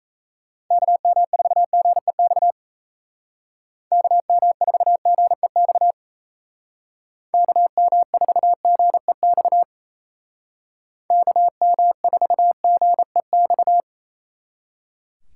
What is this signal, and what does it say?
Morse code, heard as a keyed tone, about 36 wpm, 700 Hz. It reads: KM4GEX KM4GEX KM4GEX KM4GEX